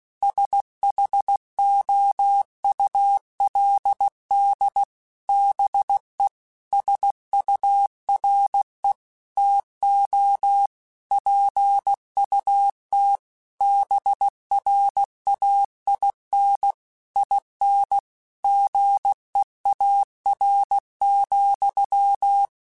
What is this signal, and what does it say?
Morse code, heard as a keyed tone, about 16 wpm, 800 Hz. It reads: SHOULD BE SURE TO PUT BRAIN IN GEAR,
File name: Chuckles / Chuckles-mp3s / BrainMouth @15